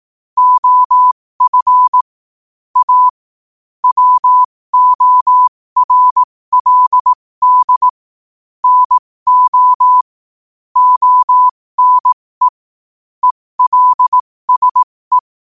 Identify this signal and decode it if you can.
Morse code, heard as a keyed tone, about 18 wpm, spaced as for 15 wpm, 1000 Hz. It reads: OF A WORLD NO ONE ELSE
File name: qotdfiles / 15wpm